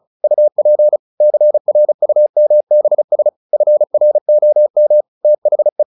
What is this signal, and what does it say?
Morse code, heard as a keyed tone, about 34 wpm, 600 Hz. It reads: UP CRUMBS FROM THE
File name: the old man and the sea 2 35wpm